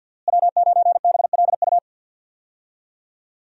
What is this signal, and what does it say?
Morse code, heard as a keyed tone, about 51 wpm, 700 Hz. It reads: W9BLU
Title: USA Call Signs - 5 Characters - Repeated 2x - 50wpm